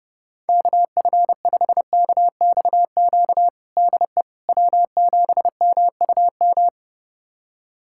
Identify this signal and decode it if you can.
Morse code, heard as a keyed tone, about 30 wpm, 700 Hz. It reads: KF5KXQ DE W7MUM
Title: USA Callsign DE Callsign 30wpm